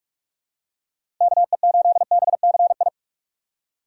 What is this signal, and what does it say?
Morse code, heard as a keyed tone, about 43 wpm, 700 Hz. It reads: KE8BCI
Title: USA Call Signs - 6 Characters - 2x Character Spacing - 45wpm